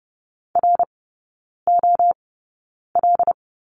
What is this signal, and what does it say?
Morse code, heard as a keyed tone, about 30 wpm, spaced as for 9 wpm, 700 Hz. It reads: ROL
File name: Lesson 18e - Letter m - Three Characters 30wpm